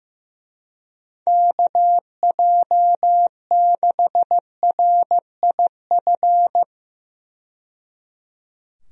Morse code, heard as a keyed tone, about 15 wpm, 700 Hz. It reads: KJ6RIF
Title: USA Call Signs - 6 Characters - 15wpm